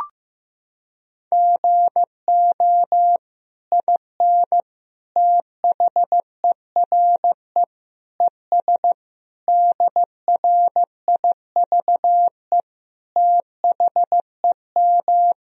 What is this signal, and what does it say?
Morse code, heard as a keyed tone, about 15 wpm, 700 Hz. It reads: GO IN THERE ES DRIVE THEM
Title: Sentences from Top 500 Words and Abbreviations 15wpm